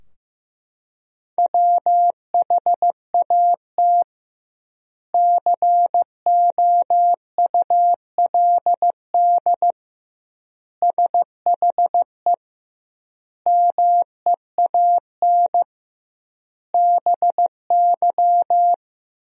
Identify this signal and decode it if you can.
Morse code, heard as a keyed tone, about 15 wpm, 700 Hz. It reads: WHAT COULD SHE MEAN BY
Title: 2x Word-Spacing - Sentences from Top 200 Words - 15wpm